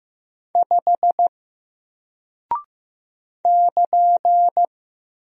Morse code, heard as a keyed tone, about 15 wpm, 700 Hz. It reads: 5 <KN>
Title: Single Character (Letter:Number:Punctuation:Prosign) Alt 15wpm